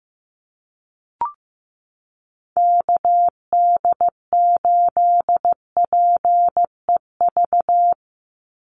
Morse code, heard as a keyed tone, about 15 wpm, 700 Hz. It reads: KD8PEV
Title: USA Call Signs 15wpm